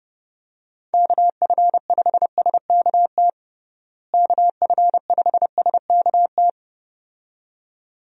Morse code, heard as a keyed tone, about 32 wpm, 700 Hz. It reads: KF5SKT KF5SKT
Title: USA Call Signs - 6 Characters - Repeated 2x - 30wpm